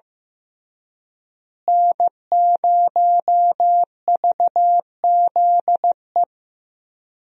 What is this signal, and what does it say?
Morse code, heard as a keyed tone, about 15 wpm, 700 Hz. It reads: N0VZE
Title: USA Call Signs - 5 Characters - Repeated 3x - 15wpm